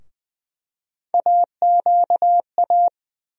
Morse code, heard as a keyed tone, about 20 wpm, 700 Hz. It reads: AQA
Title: USA Call Sign Postfixes - 3 Letters 20wpm